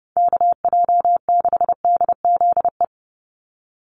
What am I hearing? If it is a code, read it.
Morse code, heard as a keyed tone, about 30 wpm, 700 Hz. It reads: KJ6DZE
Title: USA Call Signs 30wpm